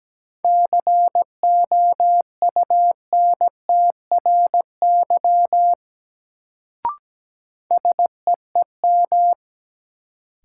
Morse code, heard as a keyed tone, about 17 wpm, 700 Hz. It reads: COUNTRY SEEM
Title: Sentences from Top 300 Words - New Word Drill - 17wpm